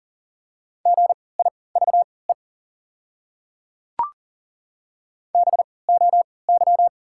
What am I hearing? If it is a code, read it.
Morse code, heard as a keyed tone, about 40 wpm, spaced as for 23 wpm, 690 Hz. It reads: GIVE BOY
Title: 3x Character Spacing - Top 200 Words - 40wpm